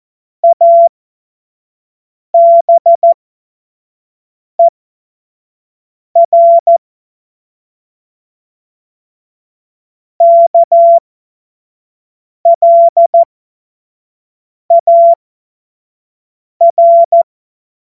Morse code, heard as a keyed tone, about 14 wpm, spaced as for 5 wpm, 680 Hz. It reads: ABER KLAR
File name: Morsekurs / Witze / Witz 09. Februar 2021, 25 F70